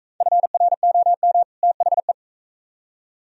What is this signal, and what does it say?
Morse code, heard as a keyed tone, about 40 wpm, 700 Hz. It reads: FROM THE